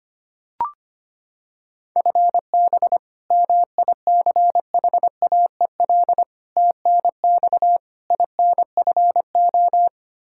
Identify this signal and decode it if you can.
Morse code, heard as a keyed tone, about 25 wpm, 700 Hz. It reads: FB MICHAEL TNX INFO
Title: Practice QSO - Exchange 3 - More Info - 25wpm